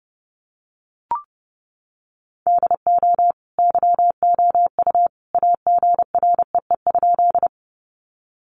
Morse code, heard as a keyed tone, about 30 wpm, 700 Hz. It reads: DO YOU AGREE?
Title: Sentences from Top 1100 Words 30wpm